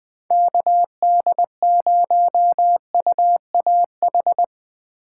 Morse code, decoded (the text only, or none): KD0UAH